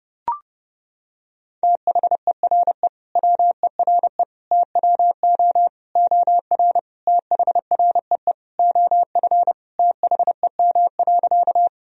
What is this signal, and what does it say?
Morse code, heard as a keyed tone, about 31 wpm, 700 Hz. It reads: THERE WERE TWO OR THREE OF THEM.
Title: Sentences from Top 1000 Words - Review Part 1 - 30wpm